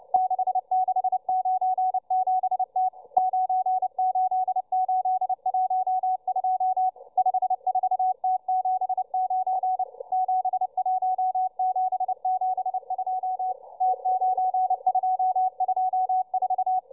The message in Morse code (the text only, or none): 6697T 98812 54T79 71772 T9224